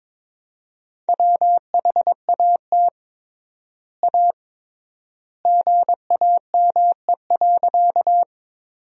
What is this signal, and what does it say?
Morse code, heard as a keyed tone, about 22 wpm, 700 Hz. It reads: WHAT A GAME.
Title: 3x Word-Spacing - Sentences from Top 500 Words - 22wpm